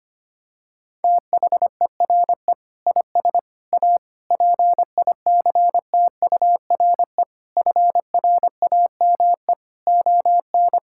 Morse code, heard as a keyed tone, about 25 wpm, 700 Hz. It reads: THERE IS A PICTURE FRAME ON
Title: Sentences from Top 1000 Words - Review Part 6 - 25wpm